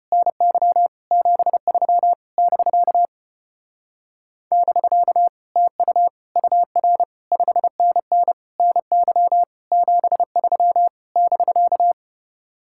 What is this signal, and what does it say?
Morse code, heard as a keyed tone, about 33 wpm, 700 Hz. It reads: NY 73 <BK> <BK> TU UR 5NN NY 73 <BK>